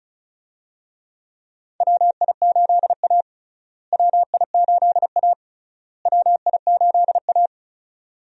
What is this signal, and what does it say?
Morse code, heard as a keyed tone, about 36 wpm, 700 Hz. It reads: WI8A WI8A WI8A
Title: USA Call Signs - 4 Characters - Repeated 3x - 35wpm